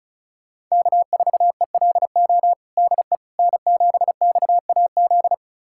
Morse code, heard as a keyed tone, about 36 wpm, 700 Hz. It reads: K4ELO DE N7XAZ
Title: USA Callsign DE Callsign 35wpm